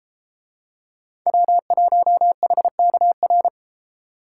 Morse code, heard as a keyed tone, about 31 wpm, 700 Hz. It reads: W1HKR